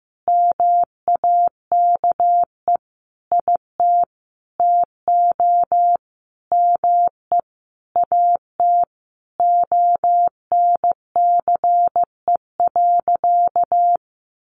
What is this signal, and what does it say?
Morse code, heard as a keyed tone, about 15 wpm, 700 Hz. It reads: MAKE IT TO ME AT ONCE.
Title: Sentences from Top 500 Words 15wpm